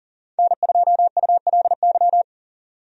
Morse code, heard as a keyed tone, about 37 wpm, 700 Hz. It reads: N2ULY
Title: USA Call Signs - 5 Characters - 40wpm